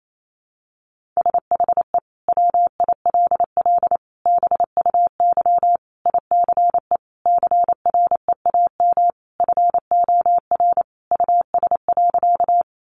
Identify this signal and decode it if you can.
Morse code, heard as a keyed tone, about 28 wpm, 700 Hz. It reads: SHE WILL BUY ICE CREAM FOR US.
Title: Sentences from Top 1500 Words 28wpm